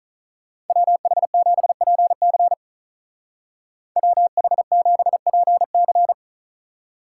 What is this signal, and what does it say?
Morse code, heard as a keyed tone, about 43 wpm, 700 Hz. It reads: WH7PC WH7PC
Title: Speed Racing - USA 2x2 Call Signs 35wpm